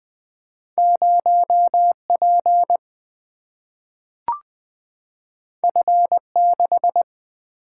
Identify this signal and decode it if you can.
Morse code, heard as a keyed tone, about 20 wpm, 700 Hz. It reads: 0P F6
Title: Single-Letter Single-Number 20wpm